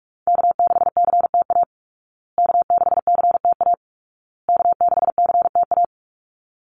Morse code, heard as a keyed tone, about 44 wpm, 700 Hz. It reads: K6CTA K6CTA K6CTA
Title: Call Signs Repeated 2x - 45wpm